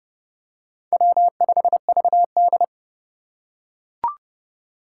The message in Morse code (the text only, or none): W5VD